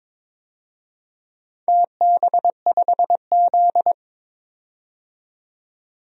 Morse code, heard as a keyed tone, about 22 wpm, 700 Hz. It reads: TB5Z
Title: DXCC-100 Call Signs - 4 Characters - 22wpm